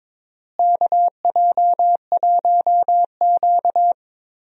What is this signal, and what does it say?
Morse code, heard as a keyed tone, about 22 wpm, 700 Hz. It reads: KJ1Q
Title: USA Call Signs - 4 Characters - Repeated 2x - 22wpm